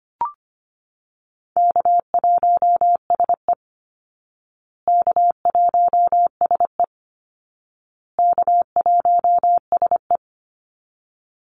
Morse code, heard as a keyed tone, about 25 wpm, 700 Hz. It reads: K1SE K1SE K1SE
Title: Call Signs Repeated 2x - 25wpm